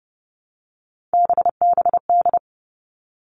Morse code, heard as a keyed tone, about 29 wpm, 700 Hz. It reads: BBD
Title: All Cut Numbers - 3-Digit 30wpm